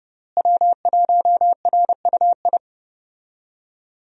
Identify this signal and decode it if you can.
Morse code, heard as a keyed tone, about 31 wpm, 700 Hz. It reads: W1RUI
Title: USA Call Signs - 5 Characters - Repeated 2x - 30wpm